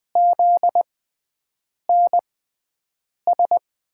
Morse code, heard as a keyed tone, about 20 wpm, spaced as for 7 wpm, 700 Hz. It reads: ZNS
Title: USA Call Signs - 6x Character Spacing - 20wpm